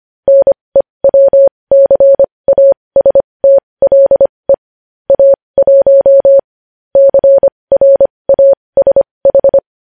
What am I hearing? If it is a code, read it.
Morse code, heard as a keyed tone, about 25 wpm, spaced as for 20 wpm, 550 Hz. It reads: NEWCASTLE A1 CRASH
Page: news headlines in morse code at 20 wpm